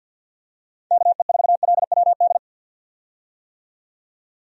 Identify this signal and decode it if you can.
Morse code, heard as a keyed tone, about 46 wpm, 700 Hz. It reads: KE4LWD
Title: USA Call Signs - 6 Characters - 50wpm